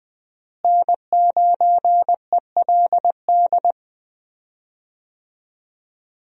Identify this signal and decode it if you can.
Morse code, heard as a keyed tone, about 21 wpm, 700 Hz. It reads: N9ELD
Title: USA Call Signs - 5 Characters - Repeated 2x - 20wpm